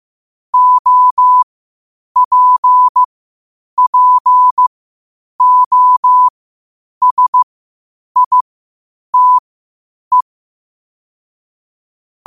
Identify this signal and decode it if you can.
Morse code, heard as a keyed tone, about 15 wpm, spaced as for 8 wpm, 1000 Hz. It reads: OPPOSITE
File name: qotdfiles / 8wpm